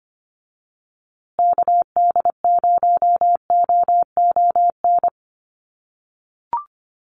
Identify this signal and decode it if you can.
Morse code, heard as a keyed tone, about 25 wpm, 700 Hz. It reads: KD0OON
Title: USA Call Signs Repeated 2x 25wpm